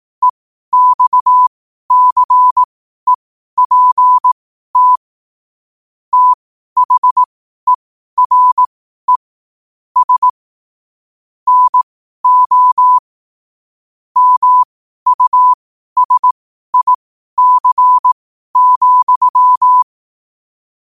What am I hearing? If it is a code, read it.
Morse code, heard as a keyed tone, about 18 wpm, spaced as for 12 wpm, 1000 Hz. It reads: EXCEPT THERE S NO MUSIC,
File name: qotdfiles / 12wpm